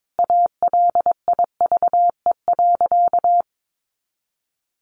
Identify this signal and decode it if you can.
Morse code, heard as a keyed tone, about 22 wpm, 700 Hz. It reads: ALIVE.
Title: A Sound Of Thunder 22wpm